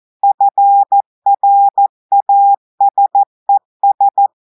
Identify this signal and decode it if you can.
Morse code, heard as a keyed tone, about 14 wpm, 800 Hz. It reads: FRASES
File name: audios / portugais / portugais texte en clair 2 vitesse 14 mots minutes